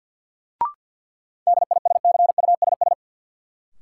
Morse code, heard as a keyed tone, about 53 wpm, 700 Hz. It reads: DISCUSS
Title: Sentences from Top 1000 Words - New Word Drill - 50wpm